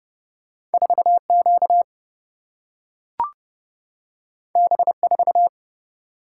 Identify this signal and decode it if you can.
Morse code, heard as a keyed tone, about 30 wpm, 700 Hz. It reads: 4Q B4
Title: Single-Letter Single-Number 30wpm